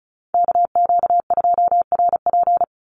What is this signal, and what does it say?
Morse code, heard as a keyed tone, about 34 wpm, 700 Hz. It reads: KQ2RP
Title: NA SOTA Chaser Callsigns 35wpm